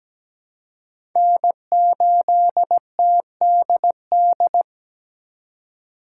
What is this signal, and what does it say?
Morse code, heard as a keyed tone, about 20 wpm, 700 Hz. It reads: N8TDD
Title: Speed Racing - USA 1x3 Call Signs 17wpm